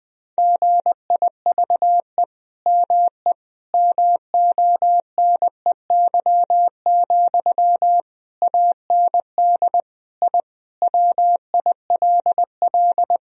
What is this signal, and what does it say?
Morse code, heard as a keyed tone, about 20 wpm, 700 Hz. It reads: GIVE ME MONEY, AND I WILL
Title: Sentences from Top 1000 Words - Review Part 2 - 20wpm